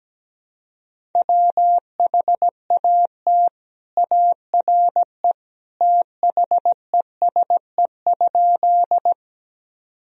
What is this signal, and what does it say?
Morse code, heard as a keyed tone, about 17 wpm, 700 Hz. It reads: WHAT ARE THESE?
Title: Sentences from Top 1000 Words - Review Part 1 - 17wpm